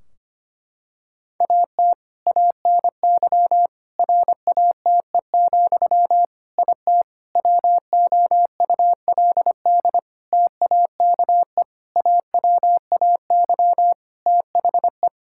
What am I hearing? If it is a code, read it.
Morse code, heard as a keyed tone, about 25 wpm, 700 Hz. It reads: AT ANY RATE, IT WOULD TAKE AWAY THE
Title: Sentences from Top 1600 Words 25wpm